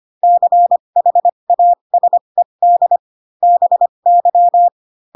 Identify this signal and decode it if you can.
Morse code, heard as a keyed tone, about 25 wpm, spaced as for 22 wpm, 700 Hz. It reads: CHASED BY
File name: Pinocchio / Pinocchio ch4 18 25 ik1hge